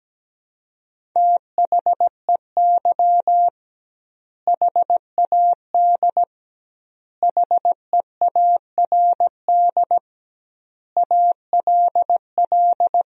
Morse code, heard as a keyed tone, about 17 wpm, 700 Hz. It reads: THEY HAD HEARD ALL
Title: 2x Word-Spacing - Sentences from Top 400 Words - 17wpm